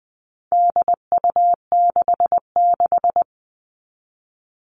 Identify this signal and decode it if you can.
Morse code, heard as a keyed tone, about 20 wpm, 700 Hz. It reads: DU66
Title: All Cut Numbers - 4-Digit 20wpm